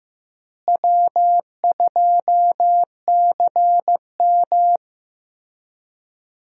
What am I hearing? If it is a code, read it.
Morse code, heard as a keyed tone, about 15 wpm, 700 Hz. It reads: W2CM
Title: USA Call Signs - 4 Characters - Repeated 2x - 15wpm